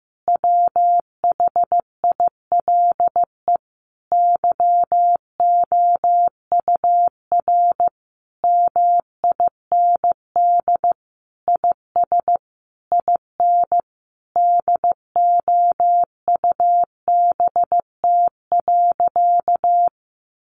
Morse code, heard as a keyed tone, about 14 wpm, 700 Hz. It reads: WHILE YOUR MIND IS IN DOUBT.
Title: Sentences from Top 1500 Words 15wpm